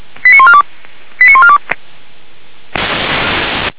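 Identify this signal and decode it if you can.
Five-tone selective call (ZVEI), heard as quick runs of five tones, tones of about 70 ms. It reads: 89142 89142
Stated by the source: ZVEI five-tone selective call 89496 received via SDR, on the feuerwehr-orlamuende.org fire brigade server.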